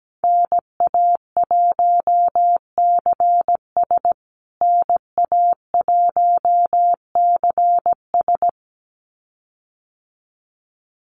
Morse code, heard as a keyed tone, about 17 wpm, 700 Hz. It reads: NA1CS NA1CS
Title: USA Call Signs Repeated 2x 17wpm